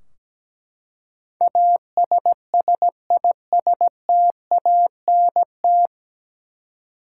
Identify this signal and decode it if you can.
Morse code, heard as a keyed tone, about 17 wpm, 700 Hz. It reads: ASSISTANT
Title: Practice QSO - Exchange 3 - More Info - 17wpm